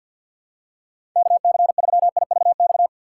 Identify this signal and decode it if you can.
Morse code, heard as a keyed tone, about 47 wpm, 700 Hz. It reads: KC3IVX
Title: USA Call Signs - 6 Characters - 4x Character Spacing - 50wpm